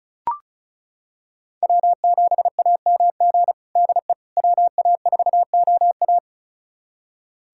Morse code, heard as a keyed tone, about 36 wpm, 700 Hz. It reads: W7AMG DE WA4OA
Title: USA Callsign DE Callsign - 5 Characters - 35wpm